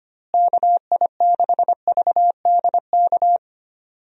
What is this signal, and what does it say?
Morse code, heard as a keyed tone, about 25 wpm, 700 Hz. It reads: KI6VDK
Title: USA Call Signs - 6 Characters - Repeated 2x - 25wpm